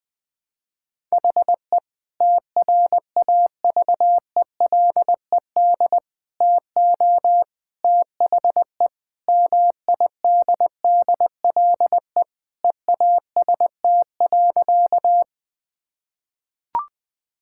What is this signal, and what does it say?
Morse code, heard as a keyed tone, about 20 wpm, 700 Hz. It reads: HE TRAVELED TO THE MIDDLE EAST.
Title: Sentences with 3-Word Phrases - Spoken Beginning - 20wpm